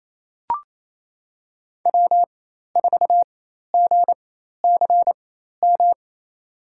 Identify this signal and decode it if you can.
Morse code, heard as a keyed tone, about 28 wpm, spaced as for 13 wpm, 700 Hz. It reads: W4GCM
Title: USA Call Signs - 5 Characters - 4x Character Spacing - 28wpm